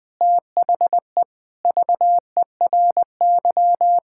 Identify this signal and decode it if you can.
Morse code, heard as a keyed tone, about 20 wpm, 700 Hz. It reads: THE VERY
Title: Sentences with 6-Word Phrases - 20wpm